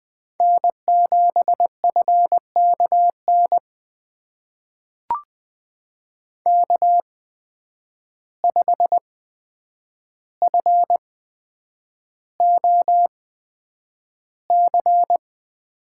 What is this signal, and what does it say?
Morse code, heard as a keyed tone, about 20 wpm, spaced as for 9 wpm, 700 Hz. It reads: N7FKN K5FOC
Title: USA Call Signs - 5 Characters - 8x Character Spacing - 20wpm